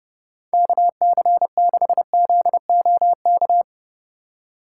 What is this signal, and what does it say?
Morse code, heard as a keyed tone, about 37 wpm, 700 Hz. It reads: KC6ZOK
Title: Speed Racing - USA 2x3 Call Signs 30wpm